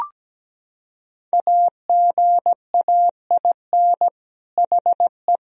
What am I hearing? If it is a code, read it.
Morse code, heard as a keyed tone, about 17 wpm, 700 Hz. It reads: AGAIN HE
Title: Sentences from Top 1000 Words - Review Part 3 - 17wpm